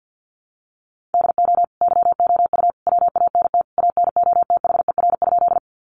Morse code, heard as a keyed tone, about 50 wpm, 700 Hz. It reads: DO YOU WANT ANOTHER?
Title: Sentences from Top 600 Words 50wpm